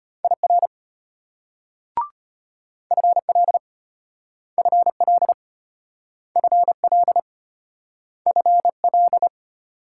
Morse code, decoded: IR FL FL FL FL

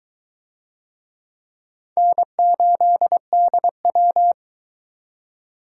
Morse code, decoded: N8DW

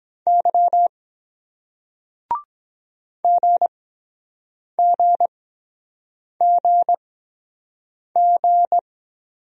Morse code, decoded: Y GGGG

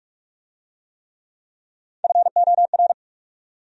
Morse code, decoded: FOR

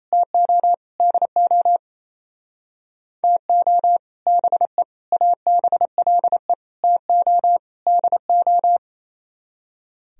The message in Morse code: TO DO TO BE ABLE TO DO

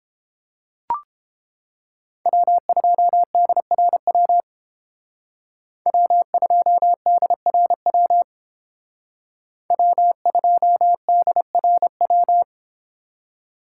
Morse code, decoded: W2DRW W2DRW W2DRW

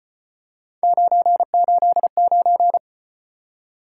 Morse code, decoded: 989

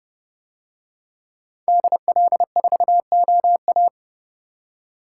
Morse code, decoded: DL4OA

